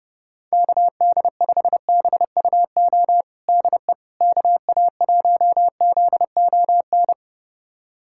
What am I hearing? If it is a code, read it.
Morse code, heard as a keyed tone, about 32 wpm, 700 Hz. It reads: KD5BUO DE KA1ZON
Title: USA Callsign DE Callsign - 6 Characters - 30wpm